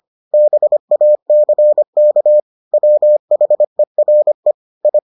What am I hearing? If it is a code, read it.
Morse code, heard as a keyed tone, about 25 wpm, 600 Hz. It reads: BACK WHERE I